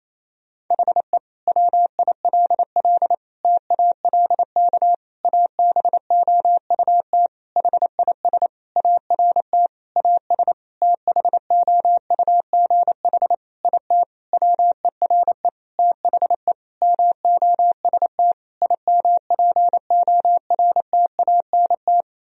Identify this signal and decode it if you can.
Morse code, heard as a keyed tone, about 28 wpm, 700 Hz. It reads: HE WILL TALK ABOUT HIS ART AS THOUGH IT WERE THE MOST IMPORTANT
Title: Sentences from Top 1000 Words - Review Part 3 - 28wpm